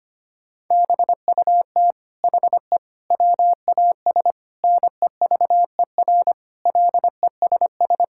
Morse code, heard as a keyed tone, about 25 wpm, 700 Hz. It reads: BUT HE WAS NEVER LESS
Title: Sentences from Top 1000 Words - Review Part 5 - 25wpm